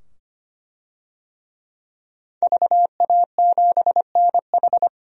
Morse code, heard as a keyed tone, about 25 wpm, 700 Hz. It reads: VA7NH
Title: DXCC-100 Call Signs - 5 Characters - 25wpm